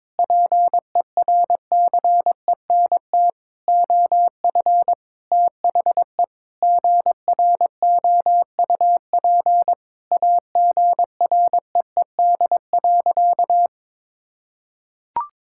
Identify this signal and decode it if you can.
Morse code, heard as a keyed tone, about 21 wpm, 700 Hz. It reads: PERCENT OF THE GROUP AGREED.
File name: practice / 22wpm / Sentences with 3-Word Phrases - 22wpm